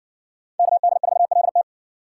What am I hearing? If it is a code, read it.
Morse code, heard as a keyed tone, about 60 wpm, 700 Hz. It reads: KD4LT